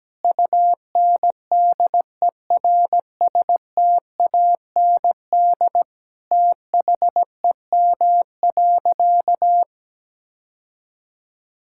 Morse code, heard as a keyed tone, about 17 wpm, 700 Hz. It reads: UNDERSTAND THEM.
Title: Sentences from Top 1000 Words - Review Part 5 - 17wpm